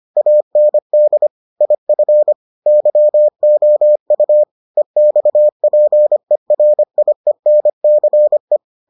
Morse code, heard as a keyed tone, about 25 wpm, 600 Hz. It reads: AND IF YOU EXPERIENCE